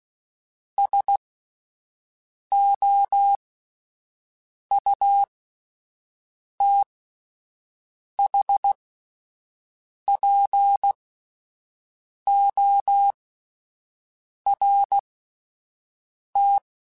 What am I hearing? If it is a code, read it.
Morse code, heard as a keyed tone, about 16 wpm, spaced as for 5 wpm, 800 Hz. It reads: SOUTHPORT